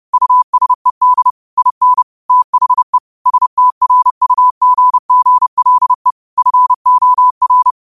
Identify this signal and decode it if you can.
Morse code, heard as a keyed tone, about 30 wpm, 1000 Hz. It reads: USED IN THE STRUGGLE FOR
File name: qotdfiles / 30wpm